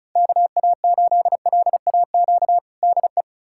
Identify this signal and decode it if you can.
Morse code, heard as a keyed tone, about 36 wpm, 700 Hz. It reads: KA8LAQ DE
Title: USA Callsign DE Callsign 35wpm